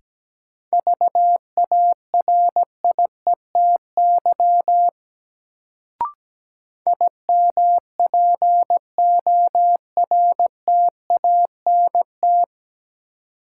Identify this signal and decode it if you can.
Morse code, heard as a keyed tone, about 17 wpm, 700 Hz. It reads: VARIETY IMPORTANT
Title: Sets of 3 Words - New Word Drill - 17wpm